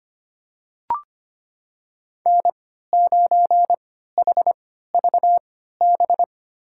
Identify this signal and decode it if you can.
Morse code, heard as a keyed tone, about 25 wpm, spaced as for 14 wpm, 700 Hz. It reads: N9HVB
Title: USA Call Signs - 5 Characters - 3x Character Spacing - 25wpm